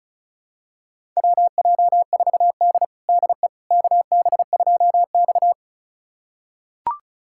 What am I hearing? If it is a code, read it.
Morse code, heard as a keyed tone, about 34 wpm, 700 Hz. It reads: WJ4D DE KB2X
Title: USA Callsign DE Callsign - 4 Characters - 35wpm